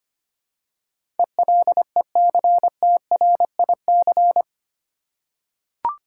Morse code, heard as a keyed tone, about 25 wpm, 700 Hz. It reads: ELECTRIC
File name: Eight Letter Words - 25wpm